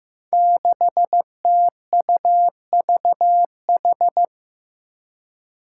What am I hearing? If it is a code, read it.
Morse code, heard as a keyed tone, about 15 wpm, 700 Hz. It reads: 6TUVH
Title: Five Challenging Letter-Number Combinations 15wpm